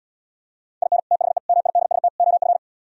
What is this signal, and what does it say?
Morse code, heard as a keyed tone, about 77 wpm, 700 Hz. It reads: IN THE MIDST OF